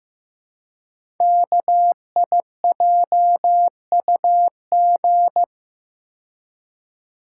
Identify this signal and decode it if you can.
Morse code, heard as a keyed tone, about 15 wpm, 700 Hz. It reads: KIJUG
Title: Code Groups - 5 Letters - 15wpm